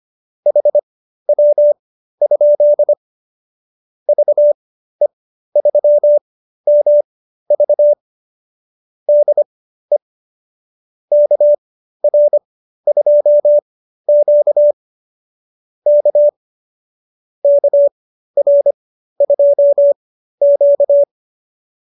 Morse code, heard as a keyed tone, about 25 wpm, spaced as for 13 wpm, 590 Hz. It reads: HW? VE3MV DE KR2Q K KR2Q